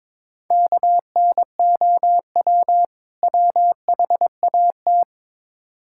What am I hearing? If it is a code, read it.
Morse code, heard as a keyed tone, about 22 wpm, 700 Hz. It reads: KNOW WHAT
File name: Build A Phrase - 2-Word Phrases - 22wpm